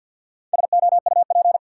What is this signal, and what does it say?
Morse code, heard as a keyed tone, about 47 wpm, 700 Hz. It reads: SOUP
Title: Sentences from Top 1000 Words - Review Part 6 - New Word Drill - 50wpm